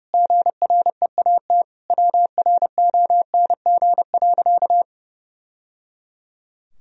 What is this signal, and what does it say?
Morse code, heard as a keyed tone, about 30 wpm, 700 Hz. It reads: GREAT WRONG.
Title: Sentences from Top 1000 Words - Review Part 4 - 30wpm